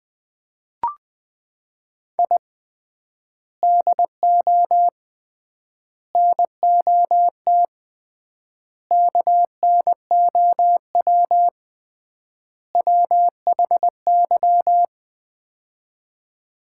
Morse code, heard as a keyed tone, about 20 wpm, 700 Hz. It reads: I DO NOT KNOW WHY
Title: 3x Word-Spacing - Sets of 5 Words - 20wpm